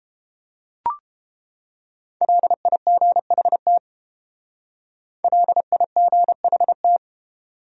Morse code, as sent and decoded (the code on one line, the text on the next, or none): .-.. .. --. .... - / .-.. .. --. .... -
LIGHT LIGHT